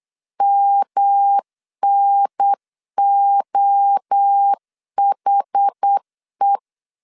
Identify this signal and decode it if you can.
Morse code, heard as a keyed tone, about 8 wpm, 800 Hz. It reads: MNOHE
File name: CW / DET / 1 3 1500C S B 8wpm 4141